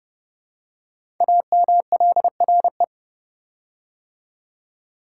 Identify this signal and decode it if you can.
Morse code, heard as a keyed tone, about 29 wpm, 700 Hz. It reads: AMLRE
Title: Code Groups - 5 Letters - 30wpm